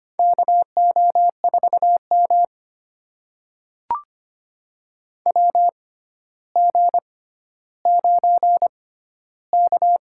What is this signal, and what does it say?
Morse code, heard as a keyed tone, about 25 wpm, spaced as for 13 wpm, 700 Hz. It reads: KO4M WG9K
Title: USA Call Signs - 6x Character Spacing - 25wpm